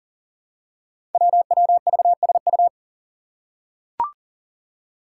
Morse code, heard as a keyed tone, about 40 wpm, 700 Hz. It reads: WWVSU
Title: Five Challenging Letter-Number Combinations 40wpm